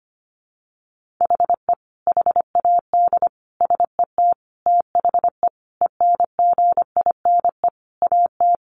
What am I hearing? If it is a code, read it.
Morse code, heard as a keyed tone, about 25 wpm, 700 Hz. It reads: HE HAD SET THE ENGINE AT
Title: Sentences from Top 1500 Words 25wpm